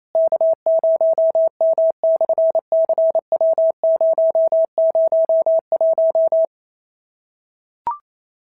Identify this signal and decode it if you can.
Morse code, heard as a keyed tone, about 28 wpm, 650 Hz. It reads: K0M/CW001
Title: SOTA QSO Practice 28wpm